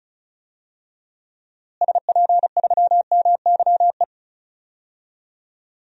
Morse code, heard as a keyed tone, about 36 wpm, 700 Hz. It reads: SP3MYE
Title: DXCC-100 Call Signs - 6 Characters - Repeated 3x - 35wpm